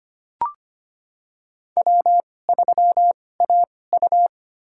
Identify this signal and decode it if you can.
Morse code, heard as a keyed tone, about 25 wpm, spaced as for 18 wpm, 700 Hz. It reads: W3AU
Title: USA Call Signs - 4 Characters - 2x Character Spacing - 25wpm